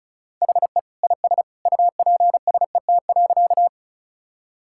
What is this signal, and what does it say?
Morse code, heard as a keyed tone, about 34 wpm, 700 Hz. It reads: HE IS UPSET.
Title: Sentences from Top 1000 Words - Review Part 5 - 35wpm